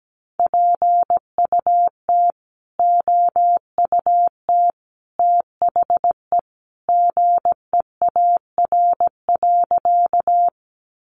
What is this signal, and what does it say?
Morse code, heard as a keyed tone, about 17 wpm, 700 Hz. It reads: PUT OUT THE GEAR.
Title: Sentences from Top 1400 Words 17wpm